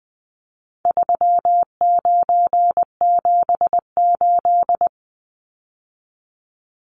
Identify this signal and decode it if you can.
Morse code, heard as a keyed tone, about 20 wpm, 700 Hz. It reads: 3978
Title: Numbers - 4-Digit 20wpm